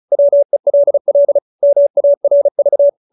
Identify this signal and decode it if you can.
Morse code, heard as a keyed tone, about 34 wpm, 570 Hz. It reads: WELL MARV
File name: QSO312 35